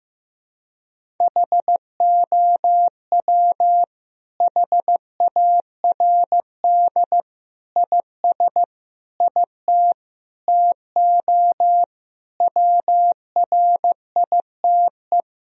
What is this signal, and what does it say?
Morse code, heard as a keyed tone, about 15 wpm, 700 Hz. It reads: HOW HARD IS IT TO WRITE
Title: Sentences from Top 1000 Words - Review Part 3 - 15wpm